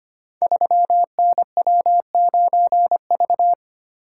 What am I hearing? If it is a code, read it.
Morse code, heard as a keyed tone, about 25 wpm, 700 Hz. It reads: 3NW9V
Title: Code Groups - 5 Letter:Numbers - 25wpm